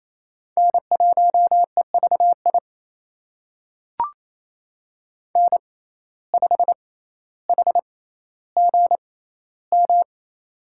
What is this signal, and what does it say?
Morse code, heard as a keyed tone, about 28 wpm, spaced as for 14 wpm, 700 Hz. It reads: N1EVI N5HGM
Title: USA Call Signs - 6x Character Spacing - 28wpm